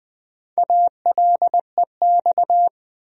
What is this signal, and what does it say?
Morse code, heard as a keyed tone, about 20 wpm, 700 Hz. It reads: ALEX
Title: USA Common Names - Max 5 Letters - 20wpm